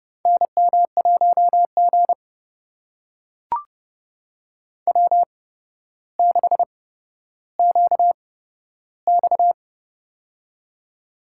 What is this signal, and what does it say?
Morse code, heard as a keyed tone, about 30 wpm, spaced as for 13 wpm, 700 Hz. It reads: NM1G W6QX
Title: USA Call Signs - 4 Characters - 8x Character Spacing - 30wpm